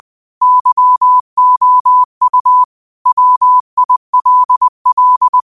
Morse code, heard as a keyed tone, about 20 wpm, 1000 Hz. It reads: YOU WILL
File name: qotdfiles / 20wpm